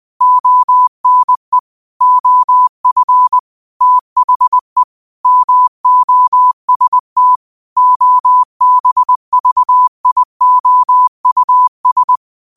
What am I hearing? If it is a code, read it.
Morse code, heard as a keyed tone, about 20 wpm, 1000 Hz. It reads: ONE OF THE MOST OBVIOUS
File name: qotdfiles / 20wpm